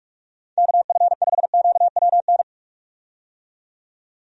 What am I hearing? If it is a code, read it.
Morse code, heard as a keyed tone, about 41 wpm, 700 Hz. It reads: KF5QWN
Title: USA Call Signs - 6 Characters - 45wpm